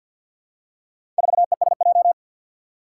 Morse code, heard as a keyed tone, about 53 wpm, 700 Hz. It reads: 4ESJ
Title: Four Challenging Letter-Number Combinations 50wpm